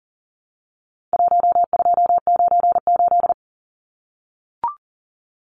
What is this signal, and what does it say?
Morse code, heard as a keyed tone, about 40 wpm, 700 Hz. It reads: 1298
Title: Numbers - 4-Digit 40wpm